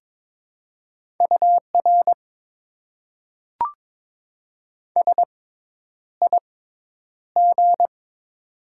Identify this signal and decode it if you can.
Morse code, heard as a keyed tone, about 22 wpm, spaced as for 10 wpm, 700 Hz. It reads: UR SIG
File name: Top 90 QSO Elements - 6x Character Spacing - 22wpm